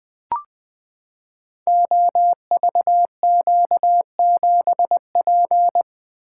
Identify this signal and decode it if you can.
Morse code, heard as a keyed tone, about 20 wpm, 700 Hz. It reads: OVQ7P
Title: Code Groups - 5 Letter:Numbers - 20wpm